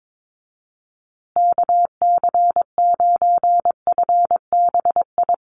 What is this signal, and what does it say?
Morse code, heard as a keyed tone, about 22 wpm, 700 Hz. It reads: KC9FBI
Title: USA Call Signs Repeated 2x 22wpm